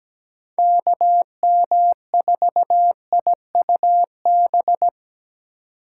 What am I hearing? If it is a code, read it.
Morse code, heard as a keyed tone, about 17 wpm, 700 Hz. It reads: KM4IUB
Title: USA Call Signs - 6 Characters - 4x Character Spacing - 17wpm